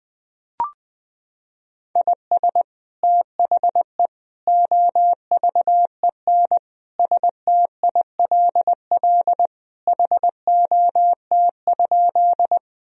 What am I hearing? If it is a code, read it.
Morse code, heard as a keyed tone, about 20 wpm, 700 Hz. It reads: IS THE OVEN STILL HOT?
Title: Sentences from Top 1000 Words - Review Part 4 - 20wpm